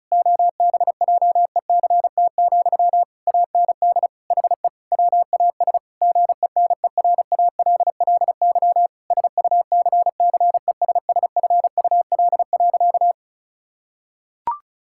Morse code, heard as a keyed tone, about 34 wpm, 690 Hz. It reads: OBJECT, AND HE WAS GENERALLY SUCCESSFUL.
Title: Sentences from Top 1000 Words - Review Part 6 - 35wpm